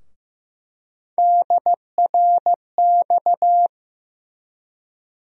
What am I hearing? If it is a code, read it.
Morse code, heard as a keyed tone, about 15 wpm, 700 Hz. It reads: DRX
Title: USA Call Sign Postfixes - 3 Letters 15wpm